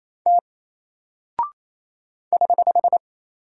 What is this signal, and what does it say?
Morse code, heard as a keyed tone, about 28 wpm, 700 Hz. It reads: T <HH>
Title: Single Character (Letter:Number:Punctuation:Prosign) 28wpm